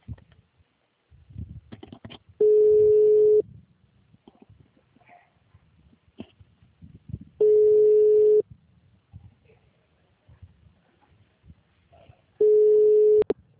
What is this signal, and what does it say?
Ringback tone (Europe, 425 Hz).
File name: out- -2005- - - .65561